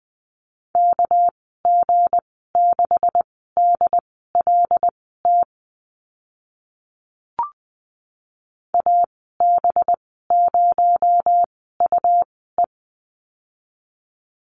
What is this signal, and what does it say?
Morse code, heard as a keyed tone, about 20 wpm, spaced as for 15 wpm, 700 Hz. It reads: KG6DLT AB0UE
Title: 238 Lesson 38g - Letter q - Call Signs 20wpm